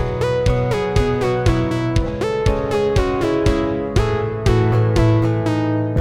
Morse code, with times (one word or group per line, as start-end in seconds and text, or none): none